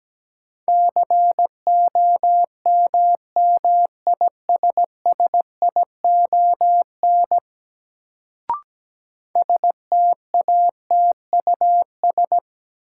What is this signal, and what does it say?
Morse code, heard as a keyed tone, about 17 wpm, 700 Hz. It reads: COMMISSION STATUS
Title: Sentences from Top 1400 Words - New Word Drill Alt - 17wpm